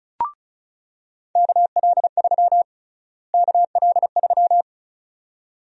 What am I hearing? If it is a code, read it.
Morse code, heard as a keyed tone, about 34 wpm, 700 Hz. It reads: KL3 KL3
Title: USA Call Sign Prefixes - Repeated 2x - 35wpm